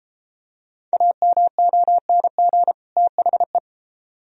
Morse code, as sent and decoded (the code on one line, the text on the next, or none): .- -- --- -. --. / - .... .
AMONG THE